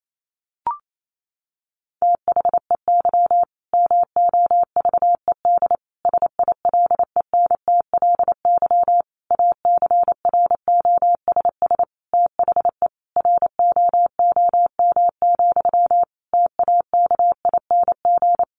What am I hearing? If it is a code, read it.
Morse code, heard as a keyed tone, about 28 wpm, 700 Hz. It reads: THEY MOVED SILENTLY ACROSS THE ROOM, TAKING
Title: A Sound Of Thunder 28wpm